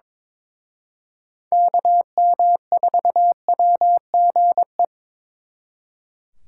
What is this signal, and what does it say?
Morse code, heard as a keyed tone, about 26 wpm, 700 Hz. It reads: KM4WGE
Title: Speed Racing - USA 2x3 Call Signs 22wpm